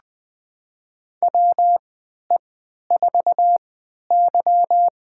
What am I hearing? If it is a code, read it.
Morse code, heard as a keyed tone, about 20 wpm, spaced as for 11 wpm, 700 Hz. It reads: WE4Y
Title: USA Call Signs - 4 Characters - 3x Character Spacing - 20wpm